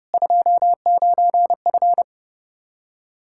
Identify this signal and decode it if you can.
Morse code, heard as a keyed tone, about 32 wpm, 700 Hz. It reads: 29F
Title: K1USN SST QSO Practice 30wpm